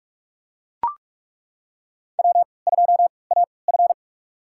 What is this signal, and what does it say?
Morse code, heard as a keyed tone, about 41 wpm, spaced as for 25 wpm, 700 Hz. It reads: W2AF
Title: USA Call Signs - 3x Character Spacing - 45wpm